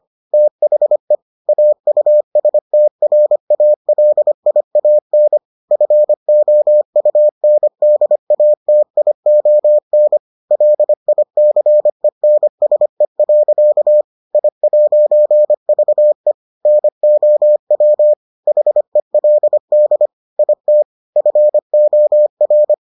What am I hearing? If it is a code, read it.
Morse code, heard as a keyed tone, about 25 wpm, 600 Hz. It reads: THE AUSTRALIAN FOUNDATION LICENSE. I'VE NOW HELD IT FOR